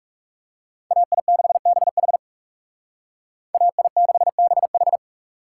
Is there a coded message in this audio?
Yes: Morse code from 0.9 to 5.0 s, heard as a keyed tone, about 39 wpm, 700 Hz. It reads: AI6BH AI6BH